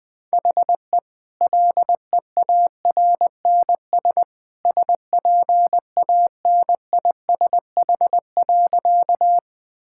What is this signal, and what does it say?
Morse code, heard as a keyed tone, about 20 wpm, 700 Hz. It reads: HE LEARNS SPANISH.
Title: Easy Sentences with Hints - 20wpm